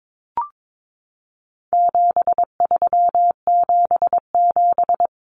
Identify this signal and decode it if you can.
Morse code, heard as a keyed tone, about 22 wpm, 700 Hz. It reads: 7377
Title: Common Cut Numbers - 4-Digit 22wpm